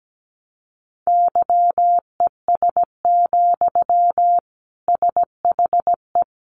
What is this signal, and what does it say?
Morse code, heard as a keyed tone, about 17 wpm, 700 Hz. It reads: YES, SHE
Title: Sentences from Top 500 Words 17wpm